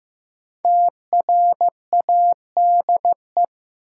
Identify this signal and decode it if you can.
Morse code, heard as a keyed tone, about 15 wpm, 700 Hz. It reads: TRADE
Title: Sentences from Top 600 Words - New Word Drill - 15wpm